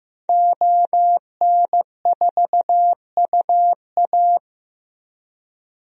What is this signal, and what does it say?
Morse code, heard as a keyed tone, about 15 wpm, 700 Hz. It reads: ON4UA
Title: DXCC-100 Call Signs - 5 Characters - 15wpm